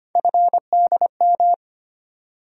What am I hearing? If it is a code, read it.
Morse code, heard as a keyed tone, about 25 wpm, 700 Hz. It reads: FDM
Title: USA Call Sign Postfixes - 3 Letters 25wpm